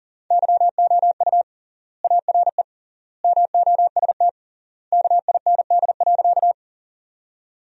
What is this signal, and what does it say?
Morse code, heard as a keyed tone, about 37 wpm, 700 Hz. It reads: YOU ARE MOST KIND.
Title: 3x Word-Spacing - Sentences from Top 200 Words - 40wpm